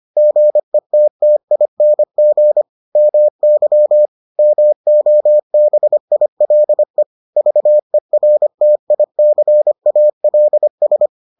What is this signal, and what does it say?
Morse code, heard as a keyed tone, about 25 wpm, 600 Hz. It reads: GETTING MY MOBILE VERTICALS